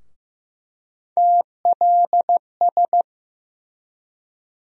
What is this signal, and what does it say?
Morse code, heard as a keyed tone, about 15 wpm, 700 Hz. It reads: TLS
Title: USA Call Sign Postfixes - 3 Letters 15wpm